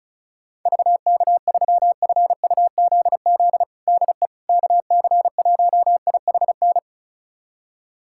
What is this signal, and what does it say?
Morse code, heard as a keyed tone, about 36 wpm, 700 Hz. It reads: VK3FUZZ DE KC1IHN
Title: USA Callsign DE Callsign 35wpm